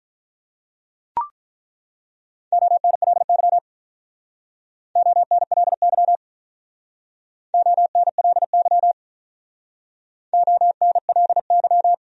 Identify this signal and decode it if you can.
Morse code, heard as a keyed tone, about 48 wpm, 700 Hz. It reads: ONLY ONLY ONLY ONLY